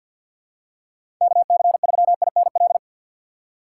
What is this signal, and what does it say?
Morse code, heard as a keyed tone, about 46 wpm, 700 Hz. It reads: KC3INL
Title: USA Call Signs - 4x Character Spacing - 50wpm